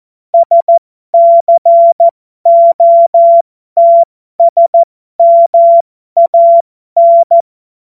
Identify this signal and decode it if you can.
Morse code, heard as a keyed tone, about 14 wpm, spaced as for 12 wpm, 680 Hz. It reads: SCOTSMAN
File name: Morsekurs / Witze